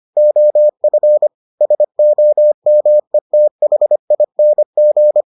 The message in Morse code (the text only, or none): OF SOMETHING